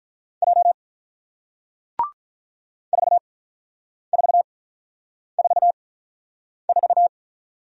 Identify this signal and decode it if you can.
Morse code, heard as a keyed tone, about 48 wpm, spaced as for 9 wpm, 690 Hz. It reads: J 4444